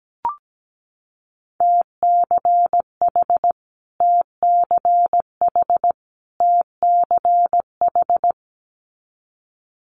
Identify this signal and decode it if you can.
Morse code, heard as a keyed tone, about 17 wpm, 690 Hz. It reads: TCH TCH TCH
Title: Multi-letter Phonemes Repeated 3x 17wpm